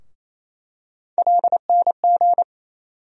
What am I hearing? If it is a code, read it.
Morse code, heard as a keyed tone, about 28 wpm, 700 Hz. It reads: LNG
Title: USA Call Sign Postfixes - 3 Letters 28wpm